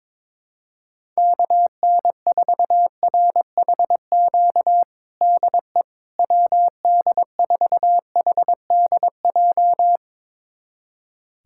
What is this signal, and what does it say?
Morse code, heard as a keyed tone, about 22 wpm, 700 Hz. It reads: KN4RHQ DE WD4HDJ
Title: USA Callsign DE Callsign - 6 Characters - 22wpm